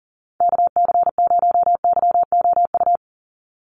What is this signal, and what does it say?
Morse code, heard as a keyed tone, about 40 wpm, 700 Hz. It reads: KC0YOU
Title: USA Call Signs 40wpm